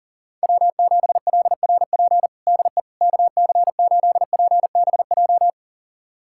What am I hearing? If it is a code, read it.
Morse code, heard as a keyed tone, about 40 wpm, 700 Hz. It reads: W7LRP DE KC8PBJ
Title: USA Callsign DE Callsign 40wpm